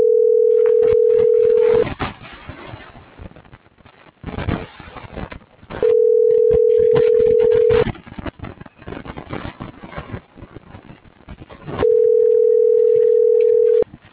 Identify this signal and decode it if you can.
Ringback tone (North America), 440 + 480 Hz.